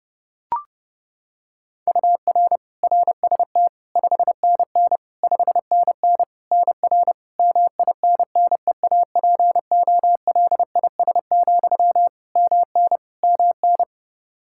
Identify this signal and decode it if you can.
Morse code, heard as a keyed tone, about 30 wpm, 700 Hz. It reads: UR RST 5NN 5NN NR MINNEAPOLIS, MN MN
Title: Practice QSO - Exchange 1 - Beginning - 30wpm